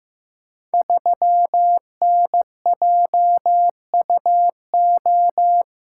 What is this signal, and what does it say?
Morse code, heard as a keyed tone, about 15 wpm, 700 Hz. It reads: 3NJUO
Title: Code Groups - 5 Letter:Numbers - 15wpm